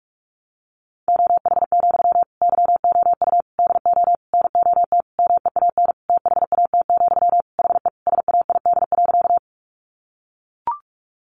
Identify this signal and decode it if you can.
Morse code, heard as a keyed tone, about 44 wpm, 700 Hz. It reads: OH, YOU DO NOT MEAN THAT, HE SAID.
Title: Sentences from Top 600 Words 45wpm